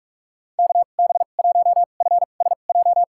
Morse code, heard as a keyed tone, about 46 wpm, spaced as for 32 wpm, 700 Hz. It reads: KB1FSJ
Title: USA Call Signs - 6 Characters - 2x Character Spacing - 45wpm